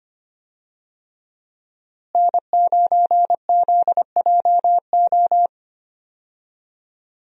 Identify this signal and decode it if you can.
Morse code, heard as a keyed tone, about 25 wpm, 700 Hz. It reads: N9ZJO
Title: USA Call Signs - 5 Characters - Repeated 3x - 25wpm